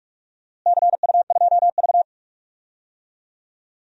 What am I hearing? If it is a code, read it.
Morse code, heard as a keyed tone, about 42 wpm, 700 Hz. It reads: CU2V
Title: DXCC-100 Call Signs - 4 Characters - Repeated 2x - 45wpm